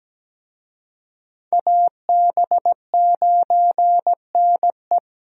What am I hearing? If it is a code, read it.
Morse code, heard as a keyed tone, about 17 wpm, 700 Hz. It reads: AB9NE
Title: USA Call Signs - 5 Characters - Repeated 3x - 17wpm